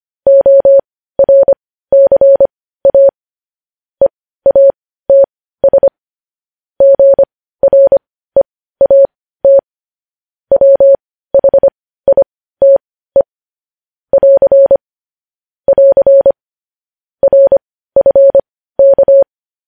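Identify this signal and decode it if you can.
Morse code, heard as a keyed tone, about 25 wpm, spaced as for 15 wpm, 550 Hz. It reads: ORCA EATS GREAT WHITE <AR> <AR> RFK
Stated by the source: Morse practice sending news headlines at 15 wpm.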